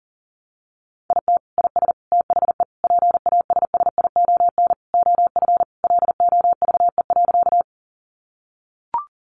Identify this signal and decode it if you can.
Morse code, heard as a keyed tone, about 40 wpm, 700 Hz. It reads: IT IS THE PASSION OF LOVE.
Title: Sentences from Top 1300 Words 40wpm